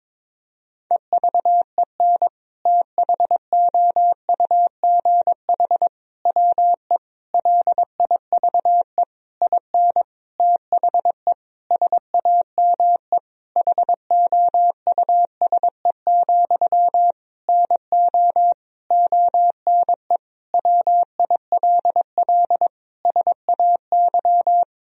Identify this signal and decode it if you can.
Morse code, heard as a keyed tone, about 22 wpm, 700 Hz. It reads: EVEN THOUGH WE LIVE IN THE SAME HOUSE, NO ONE WILL SAY
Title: Sentences from Top 1000 Words - Review Part 2 - 22wpm